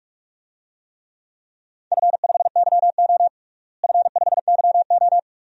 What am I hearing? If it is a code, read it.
Morse code, heard as a keyed tone, about 46 wpm, 700 Hz. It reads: F5YO F5YO
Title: DXCC-100 Call Signs - 4 Characters - Repeated 2x - 45wpm